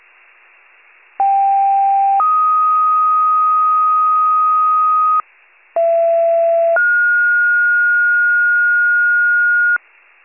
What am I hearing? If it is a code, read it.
Two-tone page, heard as a one-second tone, then a three-second tone, about 790 then 1250 Hz, about 670 then 1430 Hz. It is heard from 1.2 to 9.8 s.
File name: pager / 39 FIRE / wcalert